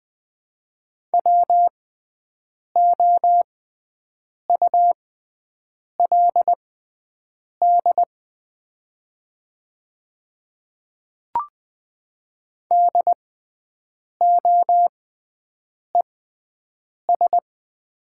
Morse code, heard as a keyed tone, about 20 wpm, spaced as for 7 wpm, 700 Hz. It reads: WOULD DOES
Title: 6x Character Spacing - Top 300 Words - 20wpm